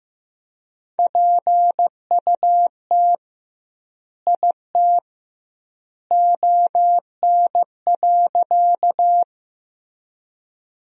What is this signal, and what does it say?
Morse code, heard as a keyed tone, about 15 wpm, 700 Hz. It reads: PUT IT ON.
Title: 2x Word-Spacing - Sentences from Top 200 Words - 15wpm